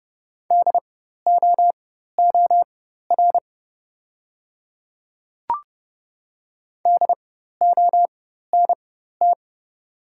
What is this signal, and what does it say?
Morse code, heard as a keyed tone, about 30 wpm, spaced as for 14 wpm, 690 Hz. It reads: DOOR DONT
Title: 4x Character Spacing - Top 300 Words - 30wpm